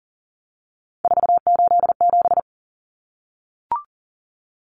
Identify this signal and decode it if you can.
Morse code, heard as a keyed tone, about 40 wpm, 700 Hz. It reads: 487
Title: Common Cut Numbers - 3-Digit 40wpm